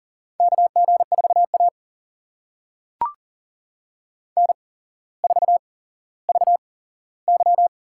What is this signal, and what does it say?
Morse code, heard as a keyed tone, about 42 wpm, spaced as for 17 wpm, 700 Hz. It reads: KG4A N4VY
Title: USA Call Signs - 4 Characters - 8x Character Spacing - 40wpm